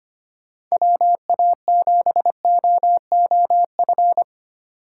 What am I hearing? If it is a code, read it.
Morse code, heard as a keyed tone, about 33 wpm, 700 Hz. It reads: WA7OOF